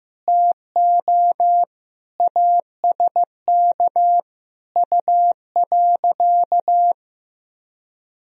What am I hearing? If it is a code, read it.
Morse code, heard as a keyed tone, about 15 wpm, 700 Hz. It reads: TO ASK U.
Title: Sentences from Top 400 Words and Abbreviations 15wpm